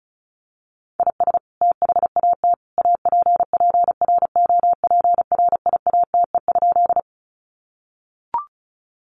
Morse code, heard as a keyed tone, about 34 wpm, 700 Hz. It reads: IS THAT APPROPRIATE?
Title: Sentences from Top 900 Words 35wpm